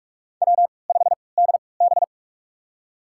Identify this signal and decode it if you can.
Morse code, heard as a keyed tone, about 48 wpm, spaced as for 26 wpm, 700 Hz. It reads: W5DB
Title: USA Call Signs - 4 Characters - 3x Character Spacing - 45wpm